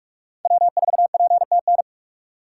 Morse code, heard as a keyed tone, about 46 wpm, 700 Hz. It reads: W4PTN
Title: USA Call Signs - 5 Characters - 3x Character Spacing - 45wpm